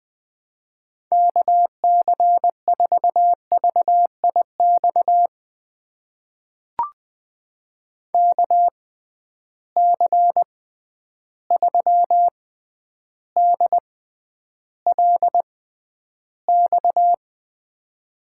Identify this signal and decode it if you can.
Morse code, heard as a keyed tone, about 20 wpm, spaced as for 10 wpm, 700 Hz. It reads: KC4VIX KC3DLX
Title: USA Call Signs - 6 Characters - 6x Character Spacing - 20wpm